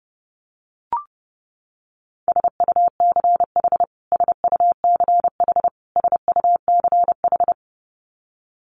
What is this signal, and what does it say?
Morse code, heard as a keyed tone, about 28 wpm, 700 Hz. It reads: SUCH SUCH SUCH
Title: Top 500 Words Repeated 30wpm